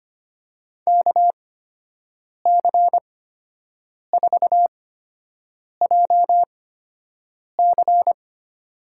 Morse code, heard as a keyed tone, about 25 wpm, spaced as for 7 wpm, 700 Hz. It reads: KC4JC
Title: USA Call Signs - 5 Characters - 8x Character Spacing - 25wpm